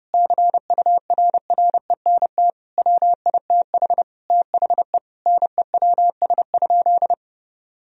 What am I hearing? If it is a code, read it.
Morse code, heard as a keyed tone, about 32 wpm, 700 Hz. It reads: CURRENT WITH THE NEWS?
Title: Sentences from Top 1000 Words - Review Part 4 - 30wpm